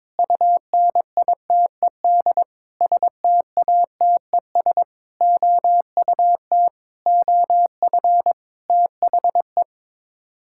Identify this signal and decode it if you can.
Morse code, heard as a keyed tone, about 22 wpm, 700 Hz. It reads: UNITED STATES OUT OF THE
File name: practice / 22wpm / Sets of 8 Words - 22wpm